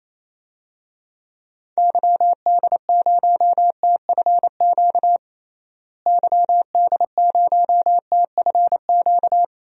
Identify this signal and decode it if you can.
Morse code, heard as a keyed tone, about 28 wpm, 700 Hz. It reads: YD0TFQ YD0TFQ
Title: DXCC-100 Call Signs - 6 Characters - Repeated 3x - 28wpm